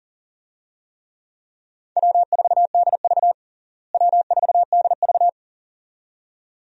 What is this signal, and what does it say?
Morse code, heard as a keyed tone, about 40 wpm, 700 Hz. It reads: W4DV W4DV
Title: USA Call Signs - 4 Characters - Repeated 2x - 40wpm